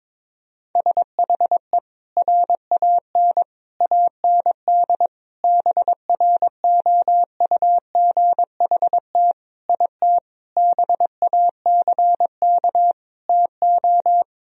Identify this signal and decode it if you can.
Morse code, heard as a keyed tone, about 22 wpm, 700 Hz. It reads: SHE RAN AND BROUGHT IT BACK TO
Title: Sentences from Top 1000 Words - Review Part 3 - 22wpm